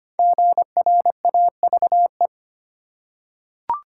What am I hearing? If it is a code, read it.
Morse code, heard as a keyed tone, about 25 wpm, 700 Hz. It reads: GRAVE
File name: 5 Letter Words - First 3 Letters Given - 25wpm